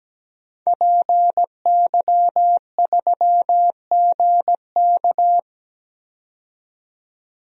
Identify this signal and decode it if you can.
Morse code, heard as a keyed tone, about 17 wpm, 700 Hz. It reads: PY3GK
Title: DXCC-100 Call Signs - 5 Characters - 17wpm